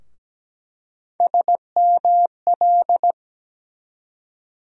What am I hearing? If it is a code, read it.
Morse code, heard as a keyed tone, about 17 wpm, 700 Hz. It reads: SML
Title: USA Call Sign Postfixes - 3 Letters 17wpm